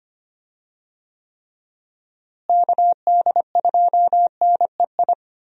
Morse code, heard as a keyed tone, about 25 wpm, 700 Hz. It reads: KD2NEI
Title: USA Call Signs - 6 Characters - Repeated 2x - 25wpm